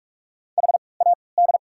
Morse code, heard as a keyed tone, about 45 wpm, spaced as for 25 wpm, 700 Hz. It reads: HAD
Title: 3x Character Spacing - Top 300 Words - 45wpm